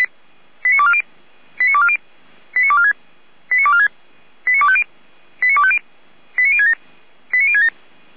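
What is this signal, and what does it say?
Five-tone selective call (ZVEI), heard as quick runs of five tones, tones of about 70 ms. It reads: 89240 89240 89246 89246 89250 89250 89067 89067
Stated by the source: ZVEI five-tone call 89160 received by SDR.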